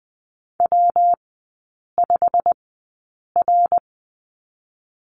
Morse code, heard as a keyed tone, about 20 wpm, spaced as for 8 wpm, 700 Hz. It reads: W5R
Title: 114 Lesson 19e - Letter w - Three Characters 20wpm